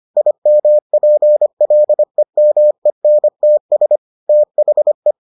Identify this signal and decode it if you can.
Morse code, heard as a keyed tone, about 25 wpm, 600 Hz. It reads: IMPLEMENTS THE